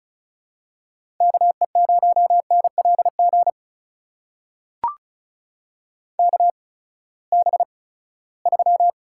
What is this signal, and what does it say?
Morse code, heard as a keyed tone, about 36 wpm, spaced as for 20 wpm, 700 Hz. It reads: KE0NLG KB3
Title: USA Call Signs - 6 Characters - 8x Character Spacing - 35wpm